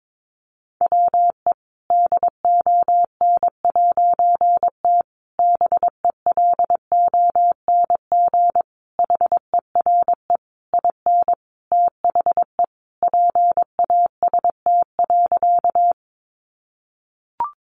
Morse code, heard as a keyed tone, about 22 wpm, 700 Hz. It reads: WE DON'T BELONG HERE IN THE PAST.
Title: A Sound Of Thunder 22wpm